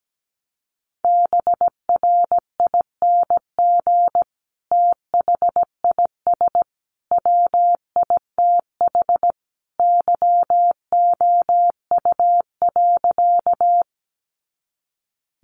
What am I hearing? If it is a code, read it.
Morse code, heard as a keyed tone, about 17 wpm, 700 Hz. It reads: BRING THIS WITH YOU.
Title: Sentences from Top 500 Words 17wpm